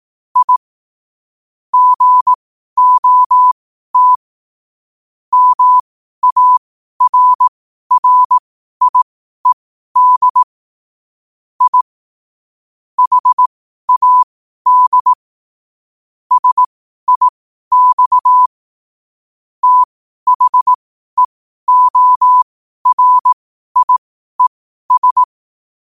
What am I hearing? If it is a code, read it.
Morse code, heard as a keyed tone, about 18 wpm, spaced as for 12 wpm, 1000 Hz. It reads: I GOT MARRIED I HAD SIX THEORIES
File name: 12wpm